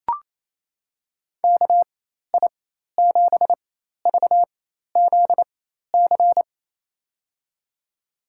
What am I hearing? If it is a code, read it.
Morse code, heard as a keyed tone, about 28 wpm, spaced as for 13 wpm, 700 Hz. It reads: KI7VZC
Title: USA Call Signs - 6 Characters - 4x Character Spacing - 28wpm